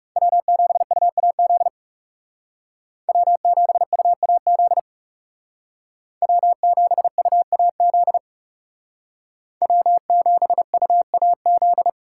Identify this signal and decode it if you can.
Morse code, heard as a keyed tone, about 39 wpm, 700 Hz. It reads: W7UAZ W7UAZ W7UAZ W7UAZ